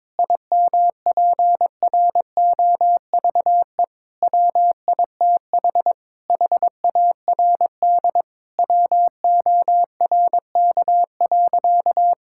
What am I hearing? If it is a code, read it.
Morse code, heard as a keyed tone, about 22 wpm, 700 Hz. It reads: IMPROVE WITH HARD WORK.
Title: Sentences from Top 1000 Words - Review Part 5 - 22wpm